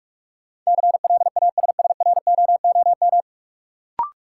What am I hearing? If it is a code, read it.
Morse code, heard as a keyed tone, about 41 wpm, 700 Hz. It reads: CLASSROOM
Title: Sentences from Top 1500 Words - New Word Drill Alt - 45wpm